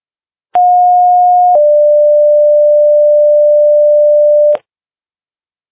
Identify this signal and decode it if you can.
Two-tone page, heard as a one-second tone, then a three-second tone, about 720 then 590 Hz. It is heard from 0.5 to 4.6 s.